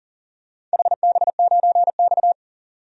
Morse code, heard as a keyed tone, about 40 wpm, 700 Hz. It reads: HB9X
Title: DXCC-100 Call Signs - 4 Characters - 40wpm